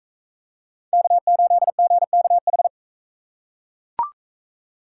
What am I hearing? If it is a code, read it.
Morse code, heard as a keyed tone, about 39 wpm, 700 Hz. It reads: K8GKH